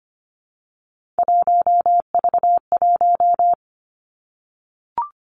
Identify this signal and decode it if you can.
Morse code, heard as a keyed tone, about 25 wpm, 700 Hz. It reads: 1V1
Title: Three Challenging Letter-Number Combinations 25wpm